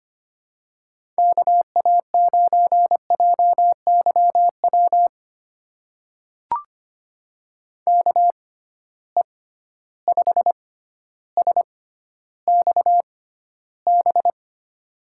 Morse code, heard as a keyed tone, about 25 wpm, spaced as for 13 wpm, 700 Hz. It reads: KA9JYW KE5SXB
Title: USA Call Signs - 6 Characters - 6x Character Spacing - 25wpm